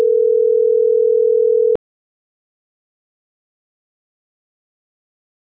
Ringback tone (North America), 440 + 480 Hz.